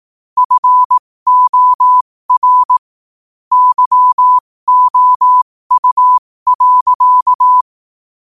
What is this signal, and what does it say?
Morse code, heard as a keyed tone, about 18 wpm, spaced as for 15 wpm, 1000 Hz. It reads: FOR YOU.
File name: qotdfiles / 15wpm